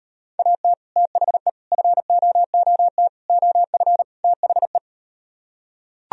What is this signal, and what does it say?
Morse code, heard as a keyed tone, about 38 wpm, 700 Hz. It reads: AT THE FOOT OF THE